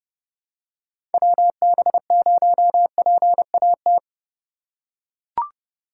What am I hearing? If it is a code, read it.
Morse code, heard as a keyed tone, about 30 wpm, 700 Hz. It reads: WB0PAT
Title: USA Call Signs - 6 Characters - 2x Character Spacing - 30wpm